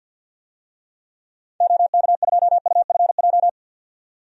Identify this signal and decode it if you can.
Morse code, heard as a keyed tone, about 46 wpm, 700 Hz. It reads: OK2UFJ
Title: DXCC-100 Call Signs - 6 Characters - 50wpm